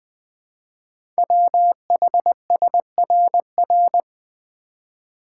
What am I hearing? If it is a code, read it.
Morse code, heard as a keyed tone, about 20 wpm, 700 Hz. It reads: WHSRR
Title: Code Groups - 5 Letters - 20wpm